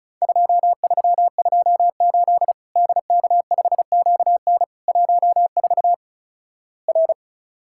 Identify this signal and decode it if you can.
Morse code, heard as a keyed tone, about 36 wpm, 700 Hz. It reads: 2328 DK5QN 14 R
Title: Worked All Europe DX Contest - QTC Exchange - 35wpm